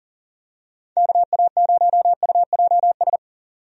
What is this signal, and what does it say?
Morse code, heard as a keyed tone, about 40 wpm, 700 Hz. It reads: KA0UJS
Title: USA Call Signs - 6 Characters - 40wpm